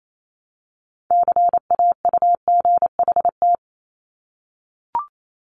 Morse code, heard as a keyed tone, about 28 wpm, 700 Hz. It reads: CAUGHT
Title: Multi-letter Phonemes 28wpm